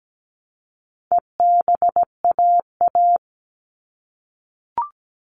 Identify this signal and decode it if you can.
Morse code, heard as a keyed tone, about 17 wpm, 700 Hz. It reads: EBAA
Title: All Cut Numbers - 4-Digit 17wpm